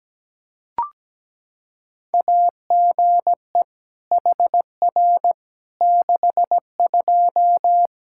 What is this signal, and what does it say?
Morse code, heard as a keyed tone, about 17 wpm, 700 Hz. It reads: AGE HR 62
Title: Practice QSO - Exchange 3 - More Info - 17wpm